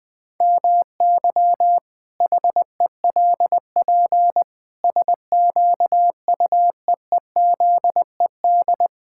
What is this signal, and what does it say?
Morse code, heard as a keyed tone, about 20 wpm, 700 Hz. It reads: MY HELP SQUEEZED
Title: Pangram Challenge v2 - 20wpm